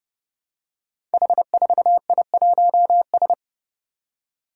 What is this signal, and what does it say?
Morse code, heard as a keyed tone, about 29 wpm, 700 Hz. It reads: H4I1S
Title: Five Challenging Letter-Number Combinations 30wpm